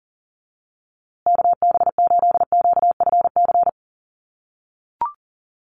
Morse code, heard as a keyed tone, about 40 wpm, 700 Hz. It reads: KB8QFC
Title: USA Call Signs Repeated 2x 40wpm